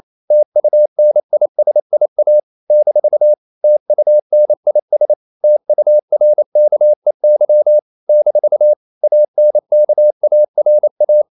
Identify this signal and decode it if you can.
Morse code, heard as a keyed tone, about 28 wpm, 600 Hz. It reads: TUNISIA - TUNIS TURKEY - ANKARA
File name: CW / Practice / NationalCapitals28